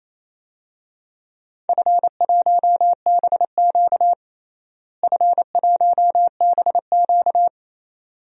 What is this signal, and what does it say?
Morse code, heard as a keyed tone, about 28 wpm, 700 Hz. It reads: F1BQ F1BQ
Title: DXCC-100 Call Signs - 4 Characters - Repeated 2x - 28wpm